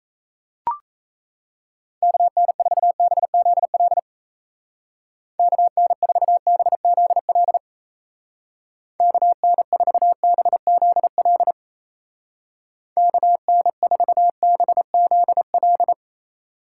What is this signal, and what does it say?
Morse code, heard as a keyed tone, about 34 wpm, 700 Hz. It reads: KN4BZL KN4BZL KN4BZL KN4BZL
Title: Speed Racing - USA 2x3 Call Signs 28wpm